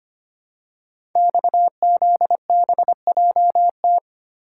Morse code, heard as a keyed tone, about 25 wpm, 700 Hz. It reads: XZBJT
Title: Code Groups - 5 Letter:Numbers - 25wpm